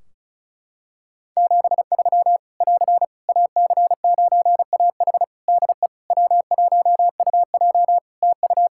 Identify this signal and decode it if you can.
Morse code, heard as a keyed tone, about 36 wpm, 700 Hz. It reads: 73 <AR> AC9AH DE W1UJ TU
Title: K1USN SST QSO Practice 35wpm